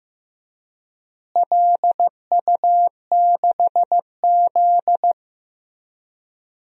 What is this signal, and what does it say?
Morse code, heard as a keyed tone, about 15 wpm, 700 Hz. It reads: LU6Z
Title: DXCC-100 Call Signs - 4 Characters - Repeated 2x - 15wpm